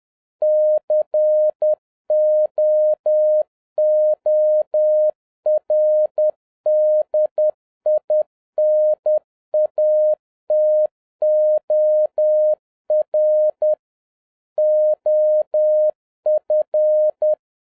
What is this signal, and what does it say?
Morse code, heard as a keyed tone, about 10 wpm, 610 Hz. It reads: COORDINATOR OF
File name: 2011-07-mp3 / n36 THE 3.11 2 10wpm